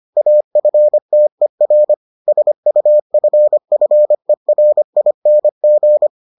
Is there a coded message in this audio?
Yes: Morse code from 0.2 to 6.1 s, heard as a keyed tone, about 25 wpm, 600 Hz. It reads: AFTER SUFFERING